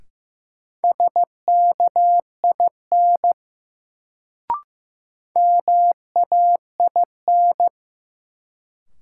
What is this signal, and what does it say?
Morse code, heard as a keyed tone, about 16 wpm, 700 Hz. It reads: SKIN MAIN
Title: Sentences from Top 1300 Words - New Word Drill Alt - 15wpm